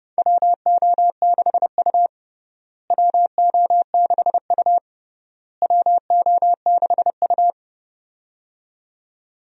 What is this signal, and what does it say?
Morse code, heard as a keyed tone, about 30 wpm, 700 Hz. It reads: WO6U WO6U WO6U
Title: USA Call Signs - 4 Characters - Repeated 3x - 30wpm